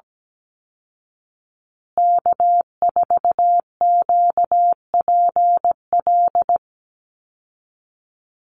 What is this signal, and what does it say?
Morse code, heard as a keyed tone, about 17 wpm, 700 Hz. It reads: K4QPL
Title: Call Signs Repeated 2x - 17wpm